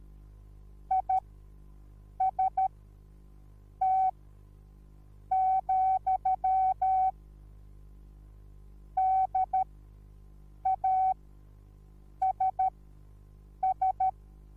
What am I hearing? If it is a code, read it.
Morse code, heard as a keyed tone, about 13 wpm, spaced as for 6 wpm, 760 Hz. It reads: IST, DASS